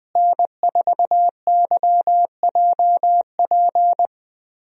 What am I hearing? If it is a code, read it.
Morse code, heard as a keyed tone, about 21 wpm, 700 Hz. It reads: N4YJP
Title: USA Call Signs - 5 Characters - Repeated 3x - 20wpm